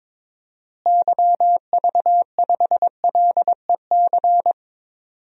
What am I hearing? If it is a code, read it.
Morse code, heard as a keyed tone, about 22 wpm, 700 Hz. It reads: YV5LEC
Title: DXCC-100 Call Signs - 6 Characters - Repeated 3x - 22wpm